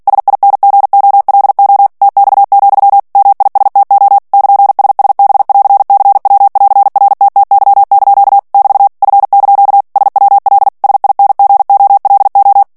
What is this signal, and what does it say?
Morse code, heard as a keyed tone, about 48 wpm, 800 Hz. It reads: SINGOLO TX, MISTO <KN>SSB<AR>CW<AR>RTTY) <BT> F) SWL SINGOLO
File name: ITALIEN texte clair 2 vitesse 45 mots minute